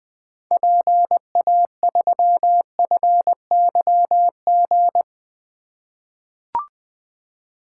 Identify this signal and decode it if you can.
Morse code, heard as a keyed tone, about 20 wpm, 700 Hz. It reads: PA3FYG
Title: 2023 SOTA Activator Call Signs - 20wpm